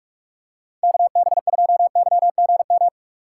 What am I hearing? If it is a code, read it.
Morse code, heard as a keyed tone, about 46 wpm, 700 Hz. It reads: KB2YGM
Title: USA Call Signs - 6 Characters - 2x Character Spacing - 45wpm